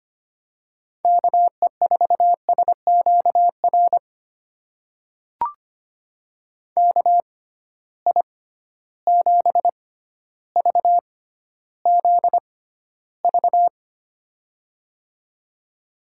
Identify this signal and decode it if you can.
Morse code, heard as a keyed tone, about 25 wpm, spaced as for 13 wpm, 700 Hz. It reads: KE4SQR KI7VZV
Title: USA Call Signs - 6 Characters - 6x Character Spacing - 25wpm